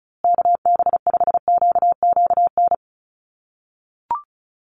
Morse code, heard as a keyed tone, about 34 wpm, 700 Hz. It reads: KB5QQN
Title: USA Call Signs Repeated 3x 35wpm